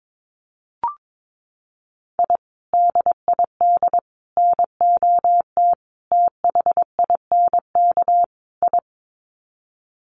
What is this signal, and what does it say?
Morse code, heard as a keyed tone, about 22 wpm, 700 Hz. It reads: I DID NOT THINK I
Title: Sets of 5 Words Encore 22wpm